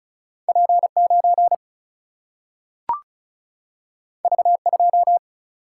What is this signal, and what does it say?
Morse code, heard as a keyed tone, about 36 wpm, 700 Hz. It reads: P9 V2
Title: Single-Letter Single-Number 35wpm